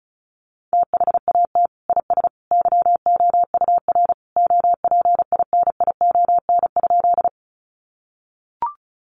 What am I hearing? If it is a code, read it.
Morse code, heard as a keyed tone, about 34 wpm, 700 Hz. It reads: THAT IS YOUR OPINION?
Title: Sentences from Top 1200 Words 35wpm